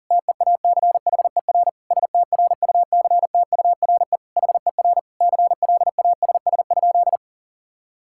Morse code, heard as a keyed tone, about 40 wpm, 700 Hz. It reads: TEACHER STRUCTURE HER CLASS?
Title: Sentences from Top 1000 Words - Review Part 6 - 40wpm